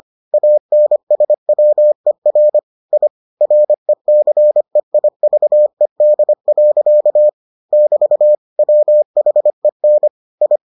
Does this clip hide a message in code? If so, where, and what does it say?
Yes: Morse code from 0.3 to 10.6 s, heard as a keyed tone, about 25 wpm, 600 Hz. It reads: ANSWER I RECEIVED. <BT> WHEN I